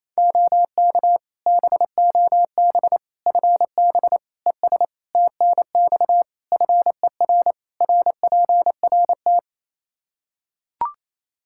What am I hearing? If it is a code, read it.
Morse code, heard as a keyed tone, about 28 wpm, 700 Hz. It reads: OK BOB FB ES TNX FER RPRT
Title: Practice QSO - Exchange 2 - Info - 28wpm